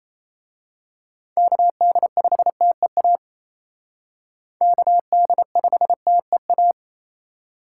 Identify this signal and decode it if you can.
Morse code, heard as a keyed tone, about 29 wpm, 700 Hz. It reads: KD5TEA KD5TEA
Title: Speed Racing - USA 2x3 Call Signs 28wpm